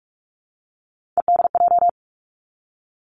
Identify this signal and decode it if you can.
Morse code, heard as a keyed tone, about 46 wpm, 700 Hz. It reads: EDJ
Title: Three Challenging Letter-Number Combinations 45wpm